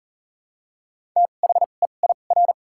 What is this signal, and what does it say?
Morse code, heard as a keyed tone, about 40 wpm, spaced as for 29 wpm, 700 Hz. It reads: THEIR
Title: 2x Character Spacing - Top 100 Words - 40wpm